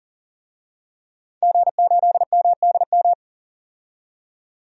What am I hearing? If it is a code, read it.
Morse code, heard as a keyed tone, about 40 wpm, 700 Hz. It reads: G8MDM
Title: DXCC-100 Call Signs - 5 Characters - 40wpm